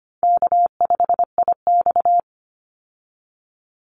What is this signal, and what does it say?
Morse code, heard as a keyed tone, about 25 wpm, 700 Hz. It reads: K5IX
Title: Call Signs Repeated 2x - 25wpm